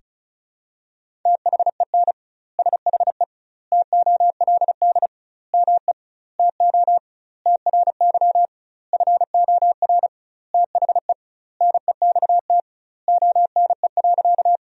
Morse code, heard as a keyed tone, about 34 wpm, 700 Hz. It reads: THEN SHE TOLD ME TO TRY FOR THE NEXT ONE.
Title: 2x Word-Spacing - Sentences from Top 400 Words - 35wpm